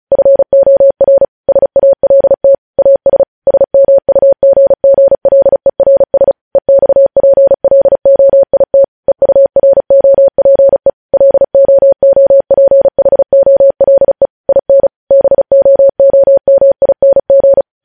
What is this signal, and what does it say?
Morse code, heard as a keyed tone, about 36 wpm, 550 Hz. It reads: FOR SALT AS SMUGGLERS EXPLOIT EUROPE LOOPHOLE IN BOOMING
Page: news headlines in morse code at 35 wpm